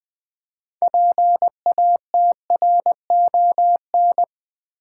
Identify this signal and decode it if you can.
Morse code, heard as a keyed tone, about 20 wpm, 700 Hz. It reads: PATRON
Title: 6 Letter Words - First 3 Letters Given - 20wpm